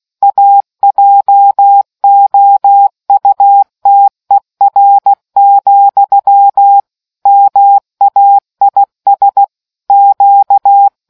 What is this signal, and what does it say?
Morse code, heard as a keyed tone, about 16 wpm, 800 Hz. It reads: AJOUTER, MAIS Q